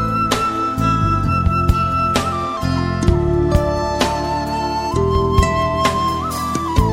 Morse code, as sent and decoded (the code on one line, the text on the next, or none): none